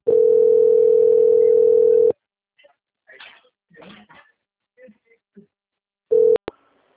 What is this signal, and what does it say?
Ringback tone (North America), 440 + 480 Hz, with speech.
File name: out- - - - - .7484